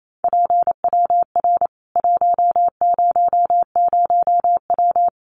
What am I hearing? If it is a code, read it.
Morse code, heard as a keyed tone, about 28 wpm, 700 Hz. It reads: PWR 100W
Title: Common QSO Phrases 28wpm